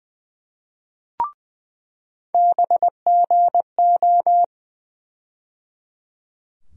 Morse code, heard as a keyed tone, about 20 wpm, 700 Hz. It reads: BGO
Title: USA Call Sign Postfixes - 3 Letters 20wpm